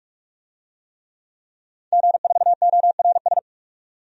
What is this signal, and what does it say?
Morse code, heard as a keyed tone, about 46 wpm, 700 Hz. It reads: G4ORS
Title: DXCC-100 Call Signs - 5 Characters - 45wpm